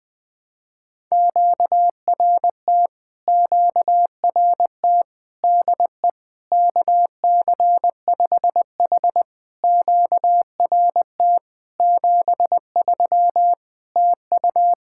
Morse code, heard as a keyed tone, about 20 wpm, 700 Hz. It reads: QRT QRT DE KC5H QRT 73 TU
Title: POTA QSO Practice - 20wpm